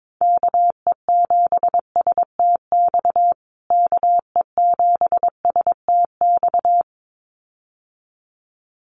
Morse code, heard as a keyed tone, about 22 wpm, 700 Hz. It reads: KE7STX KE7STX
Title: USA Call Signs Repeated 2x 22wpm